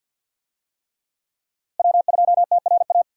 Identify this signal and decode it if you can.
Morse code, heard as a keyed tone, about 51 wpm, 700 Hz. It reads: W2TRA
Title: USA Call Signs - 5 Characters - Repeated 3x - 50wpm